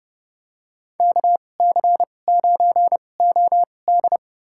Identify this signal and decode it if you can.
Morse code, heard as a keyed tone, about 30 wpm, spaced as for 22 wpm, 700 Hz. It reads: KC9OD
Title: USA Call Signs - 5 Characters - 2x Character Spacing - 30wpm